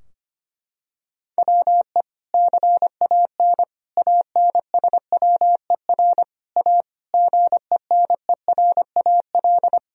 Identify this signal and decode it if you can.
Morse code, heard as a keyed tone, about 25 wpm, 700 Hz. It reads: WE CAN ANSWER A GENERAL
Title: Sentences from Top 1000 Words - Review Part 3 - 25wpm